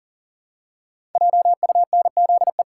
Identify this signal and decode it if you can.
Morse code, heard as a keyed tone, about 42 wpm, 690 Hz. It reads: JUNZE
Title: Code Groups - 5 Letters - 40wpm